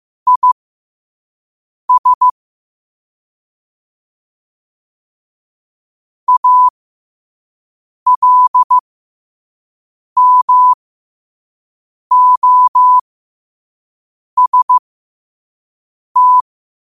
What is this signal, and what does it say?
Morse code, heard as a keyed tone, about 15 wpm, spaced as for 5 wpm, 1000 Hz. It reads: IS ALMOST
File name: qotdfiles / 5wpm